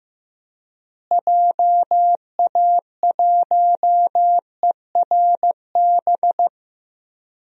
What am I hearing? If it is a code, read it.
Morse code, heard as a keyed tone, about 15 wpm, 700 Hz. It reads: JA1ERB
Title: DXCC-100 Call Signs - 6 Characters - Repeated 3x - 15wpm